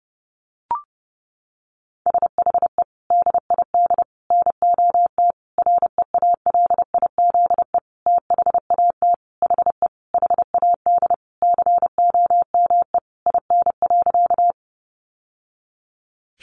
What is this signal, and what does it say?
Morse code, heard as a keyed tone, about 30 wpm, 700 Hz. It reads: SHE DID NOT REALIZE THAT HE HAD COME IN.
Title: Sentences from Top 1000 Words 30wpm